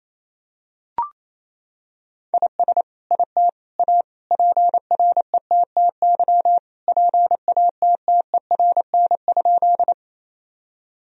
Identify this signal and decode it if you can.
Morse code, heard as a keyed tone, about 28 wpm, 700 Hz. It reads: IS IT A PRETTY PATTERN?
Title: Sentences from Top 1000 Words - Review Part 2 - 28wpm